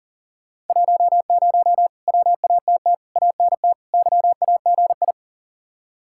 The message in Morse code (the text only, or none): 10 WATT ANT YAGI